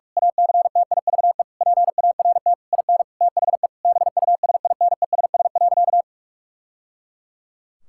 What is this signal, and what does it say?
Morse code, heard as a keyed tone, about 43 wpm, 700 Hz. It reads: ACTIVE PART IN THE BUSINESS.
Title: Sentences from Top 1000 Words - Review Part 6 - 45wpm